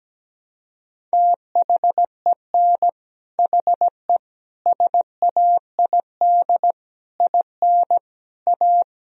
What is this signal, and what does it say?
Morse code, heard as a keyed tone, about 17 wpm, 700 Hz. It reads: THEN HE SAID IN A
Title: Sentences from Top 1000 Words - Review Part 1 - 17wpm